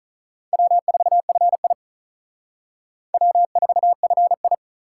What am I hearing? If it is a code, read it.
Morse code, heard as a keyed tone, about 46 wpm, 700 Hz. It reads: W4FI W4FI